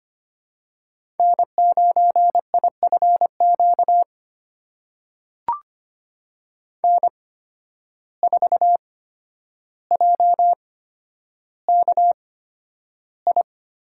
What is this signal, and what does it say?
Morse code, heard as a keyed tone, about 25 wpm, spaced as for 11 wpm, 700 Hz. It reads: N9IFQ N4JKI
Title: USA Call Signs - 5 Characters - 8x Character Spacing - 25wpm